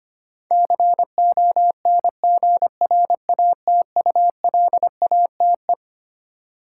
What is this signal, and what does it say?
Morse code, heard as a keyed tone, about 25 wpm, 700 Hz. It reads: CONGRATULATE
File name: Twelve Letter Words - 25wpm